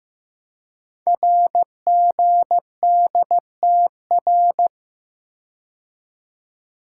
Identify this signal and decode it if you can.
Morse code, heard as a keyed tone, about 15 wpm, 700 Hz. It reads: RGDTR
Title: Code Groups - 5 Letters - 15wpm